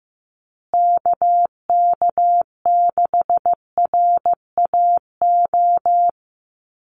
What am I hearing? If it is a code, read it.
Morse code, heard as a keyed tone, about 15 wpm, 700 Hz. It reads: KK6RAO
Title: USA Call Signs Repeated 3x 15wpm